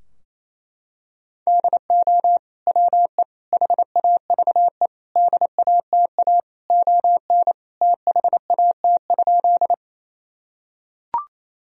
Morse code, heard as a keyed tone, about 28 wpm, 700 Hz. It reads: DO WE HAVE DATA ON THAT?
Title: Sentences from Top 1000 Words - Review Part 3 - 28wpm